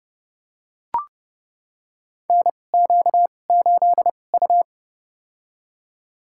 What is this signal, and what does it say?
Morse code, heard as a keyed tone, about 30 wpm, spaced as for 22 wpm, 690 Hz. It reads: NQ8U
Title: USA Call Signs - 4 Characters - 2x Character Spacing - 30wpm